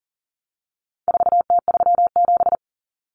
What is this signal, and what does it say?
Morse code, heard as a keyed tone, about 40 wpm, 700 Hz. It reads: 4T37
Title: Common Cut Numbers - 4-Digit 40wpm